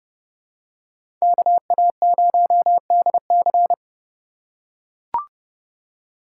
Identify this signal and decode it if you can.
Morse code, heard as a keyed tone, about 31 wpm, 700 Hz. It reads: KA0DC
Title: USA Call Signs - 2x Character Spacing - 30wpm